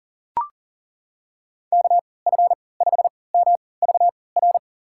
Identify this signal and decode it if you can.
Morse code, heard as a keyed tone, about 40 wpm, spaced as for 23 wpm, 700 Hz. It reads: KF5MVR
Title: USA Call Signs - 6 Characters - 3x Character Spacing - 40wpm